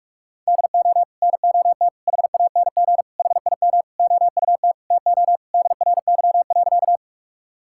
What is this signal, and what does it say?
Morse code, heard as a keyed tone, about 41 wpm, 700 Hz. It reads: DO NOT HANG HIM OUT TO DRY.
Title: Sentences from Top 1600 Words 45wpm